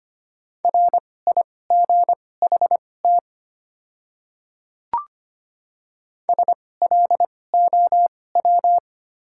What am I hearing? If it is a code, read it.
Morse code, heard as a keyed tone, about 25 wpm, spaced as for 18 wpm, 690 Hz. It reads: RIGHT SLOW
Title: 2x Character Spacing - Top 500 Words - 25wpm